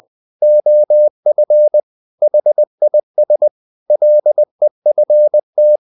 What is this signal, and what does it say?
Morse code, heard as a keyed tone, about 20 wpm, 600 Hz. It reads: OF HIS LEFT
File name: the old man and the sea part 2 20wpm